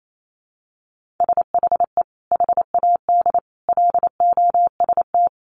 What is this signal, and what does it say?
Morse code, heard as a keyed tone, about 28 wpm, 700 Hz. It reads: SHE HAD LOST
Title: Sentences from Top 1500 Words 28wpm